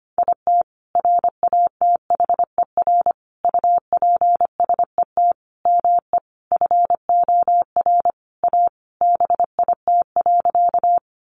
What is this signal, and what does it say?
Morse code, heard as a keyed tone, about 25 wpm, 700 Hz. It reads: IT RATHER UPSET ME FOR A BIT.
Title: Sentences from Top 900 Words 25wpm